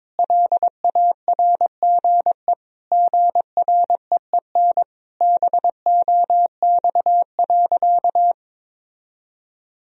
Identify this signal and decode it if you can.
Morse code, heard as a keyed tone, about 22 wpm, 700 Hz. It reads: LARGE GREEN BOX.
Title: Sentences from Top 1000 Words - Review Part 2 - 22wpm